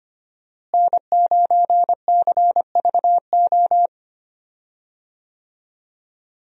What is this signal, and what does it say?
Morse code, heard as a keyed tone, about 25 wpm, 700 Hz. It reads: N9CVO
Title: USA Call Signs - 5 Characters - Repeated 3x - 25wpm